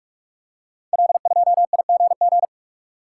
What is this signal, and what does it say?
Morse code, heard as a keyed tone, about 44 wpm, 690 Hz. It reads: L2IGG
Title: Code Groups - 5 Letter:Numbers - 45wpm